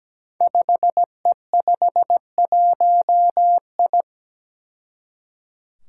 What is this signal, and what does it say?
Morse code, heard as a keyed tone, about 17 wpm, 700 Hz. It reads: 5E51I
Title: Five Challenging Letter-Number Combinations 17wpm